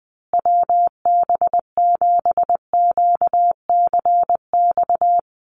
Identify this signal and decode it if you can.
Morse code, heard as a keyed tone, about 20 wpm, 700 Hz. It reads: WB7QCX
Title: USA Call Signs Repeated 2x 20wpm